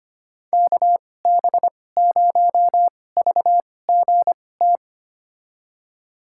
Morse code, heard as a keyed tone, about 25 wpm, spaced as for 18 wpm, 700 Hz. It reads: KB0VGT
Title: USA Call Signs - 6 Characters - 2x Character Spacing - 25wpm